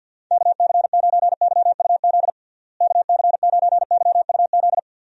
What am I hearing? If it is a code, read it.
Morse code, heard as a keyed tone, about 47 wpm, 700 Hz. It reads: KC9YUZ KC9YUZ
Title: USA Call Signs - 6 Characters - Repeated 2x - 50wpm